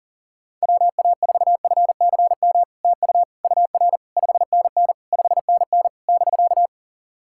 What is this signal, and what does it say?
Morse code, heard as a keyed tone, about 40 wpm, 700 Hz. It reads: WA4FCM TU UR 5NN 5NN <BK>
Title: SOTA QSO Practice 40wpm